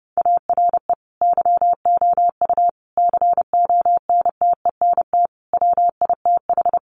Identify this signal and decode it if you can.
Morse code, heard as a keyed tone, about 30 wpm, 700 Hz. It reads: ARE YOU CONTENT WITH
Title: Sentences from Top 900 Words 30wpm